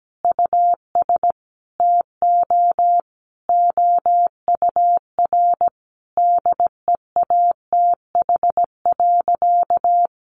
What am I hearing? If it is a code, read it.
Morse code, heard as a keyed tone, about 17 wpm, 700 Hz. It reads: US TO OUR DEATH.
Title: Sentences from Top 1200 Words 17wpm